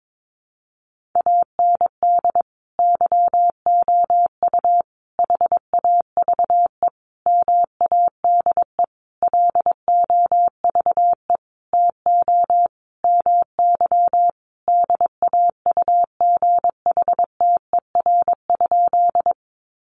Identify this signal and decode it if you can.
Morse code, heard as a keyed tone, about 22 wpm, 700 Hz. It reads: AND YOU HAVE MADE LOVE TO MY DAUGHTER?
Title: Sentences from Top 1500 Words 22wpm